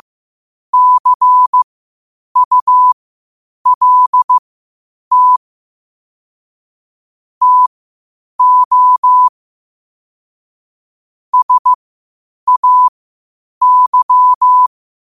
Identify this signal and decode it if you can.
Morse code, heard as a keyed tone, about 15 wpm, spaced as for 8 wpm, 1000 Hz. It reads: CULT TO SAY
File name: qotdfiles / 8wpm